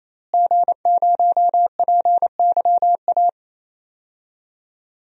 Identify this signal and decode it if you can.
Morse code, heard as a keyed tone, about 27 wpm, 700 Hz. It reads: G0PYA
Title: DXCC-100 Call Signs - 5 Characters - Repeated 3x - 28wpm